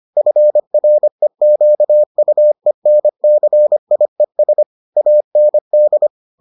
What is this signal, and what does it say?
Morse code, heard as a keyed tone, about 25 wpm, 600 Hz. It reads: FREQUENCIES AND